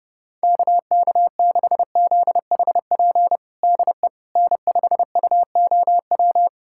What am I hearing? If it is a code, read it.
Morse code, heard as a keyed tone, about 32 wpm, 700 Hz. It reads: KK6ZHP DE N5UOW
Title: USA Callsign DE Callsign 30wpm